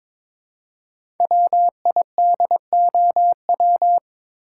Morse code, heard as a keyed tone, about 22 wpm, 700 Hz. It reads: WIDOW
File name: 5 Letter Words - First 3 Letters Given - 22wpm